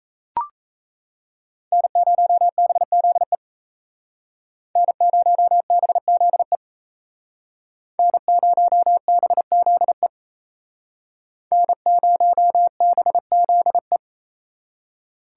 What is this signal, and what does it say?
Morse code, heard as a keyed tone, about 35 wpm, 700 Hz. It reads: N0BZE N0BZE N0BZE N0BZE
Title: Speed Racing - USA 1x3 Call Signs 28wpm